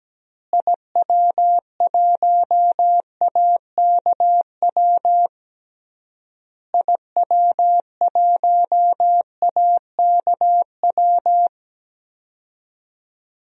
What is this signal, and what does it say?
Morse code, heard as a keyed tone, about 17 wpm, 700 Hz. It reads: IW1AKW IW1AKW
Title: DXCC-100 Call Signs - 6 Characters - Repeated 3x - 17wpm